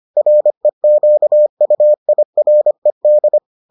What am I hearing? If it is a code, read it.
Morse code, heard as a keyed tone, about 25 wpm, 600 Hz. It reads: REQUIRED